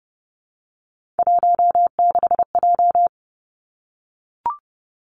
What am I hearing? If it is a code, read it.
Morse code, heard as a keyed tone, about 30 wpm, 700 Hz. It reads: 16J
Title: Three Challenging Letter-Number Combinations 30wpm